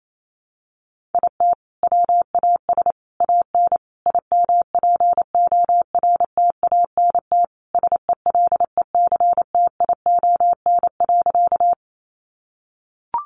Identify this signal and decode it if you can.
Morse code, heard as a keyed tone, about 28 wpm, 700 Hz. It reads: IT WAS AN IMPORTANT SELECTION.
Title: Sentences from Top 1400 Words 28wpm